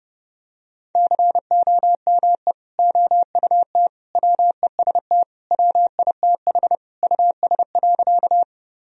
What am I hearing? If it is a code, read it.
Morse code, heard as a keyed tone, about 32 wpm, 700 Hz. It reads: COME OUT WEST WITH US.
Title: Sentences from Top 1000 Words - Review Part 2 - 30wpm